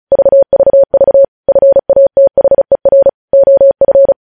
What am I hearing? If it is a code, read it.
Morse code, heard as a keyed tone, about 36 wpm, 550 Hz. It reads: VVV FATHER OF
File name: podcast / News headlines-1x-35x35